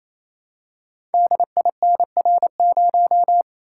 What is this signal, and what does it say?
Morse code, heard as a keyed tone, about 29 wpm, 700 Hz. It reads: DINR0
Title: Code Groups - 5 Letter:Numbers - 28wpm